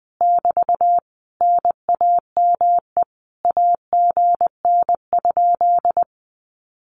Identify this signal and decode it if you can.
Morse code, heard as a keyed tone, about 20 wpm, 700 Hz. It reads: <BT> NAME AGN?
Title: Common QSO Phrases Encore 20wpm